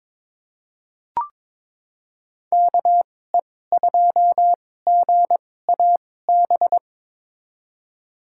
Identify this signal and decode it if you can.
Morse code, heard as a keyed tone, about 22 wpm, spaced as for 16 wpm, 700 Hz. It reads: KE2GAB
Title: USA Call Signs - 6 Characters - 2x Character Spacing - 22wpm